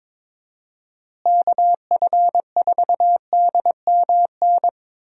Morse code, heard as a keyed tone, about 22 wpm, 700 Hz. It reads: KF4DMN
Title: USA Call Signs - 6 Characters - Repeated 3x - 22wpm